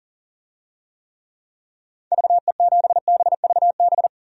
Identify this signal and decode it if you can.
Morse code, heard as a keyed tone, about 40 wpm, 700 Hz. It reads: VE7BVB
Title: DXCC-100 Call Signs - 6 Characters - 40wpm